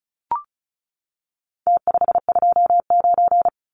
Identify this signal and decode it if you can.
Morse code, heard as a keyed tone, about 34 wpm, 700 Hz. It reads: T529
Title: Numbers - Contest Style - 4-Digit 35wpm